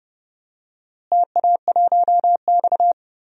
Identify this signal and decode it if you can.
Morse code, heard as a keyed tone, about 29 wpm, 700 Hz. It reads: TA1X
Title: DXCC-100 Call Signs - 4 Characters - Repeated 3x - 30wpm